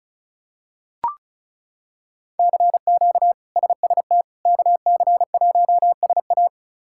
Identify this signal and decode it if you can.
Morse code, heard as a keyed tone, about 34 wpm, 700 Hz. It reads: CQ SST KC1SA
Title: K1USN SST QSO Practice - 35wpm